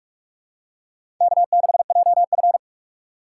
Morse code, heard as a keyed tone, about 46 wpm, 690 Hz. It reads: K6JF
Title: USA Call Signs - 4 Characters - 3x Character Spacing - 45wpm